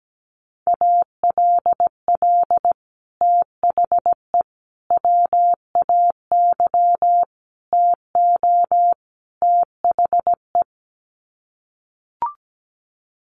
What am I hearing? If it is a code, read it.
Morse code, heard as a keyed tone, about 17 wpm, 700 Hz. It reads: ALL THE WAY TO THE
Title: Sets of 5 Words 17wpm